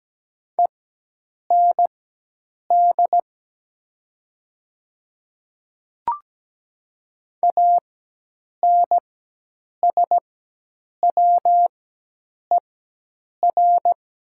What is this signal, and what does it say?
Morse code, heard as a keyed tone, about 17 wpm, spaced as for 8 wpm, 700 Hz. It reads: END ANSWER
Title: 4x Character Spacing - Top 500 Words - 17wpm